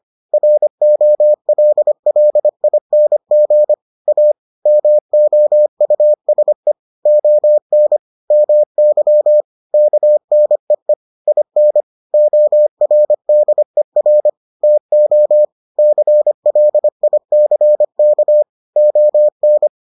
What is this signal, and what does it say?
Morse code, heard as a keyed tone, about 25 wpm, 600 Hz. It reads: ROLLING A MOUSE ON MY KNEE IN ORDER TO CLICK ON